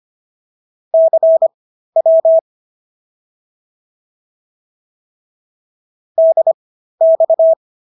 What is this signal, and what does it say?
Morse code, heard as a keyed tone, about 25 wpm, spaced as for 13 wpm, 650 Hz. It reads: CW DX